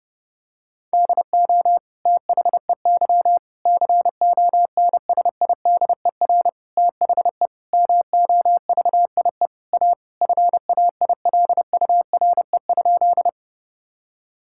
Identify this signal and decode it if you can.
Morse code, heard as a keyed tone, about 32 wpm, 700 Hz. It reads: DO THEY CONSIDER THE MOVIE A FAILURE?
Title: Sentences from Top 1000 Words - Review Part 6 - 30wpm